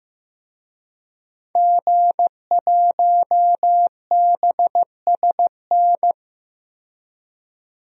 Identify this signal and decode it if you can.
Morse code, heard as a keyed tone, about 15 wpm, 700 Hz. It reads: G1BSN
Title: DXCC-100 Call Signs - 5 Characters - 15wpm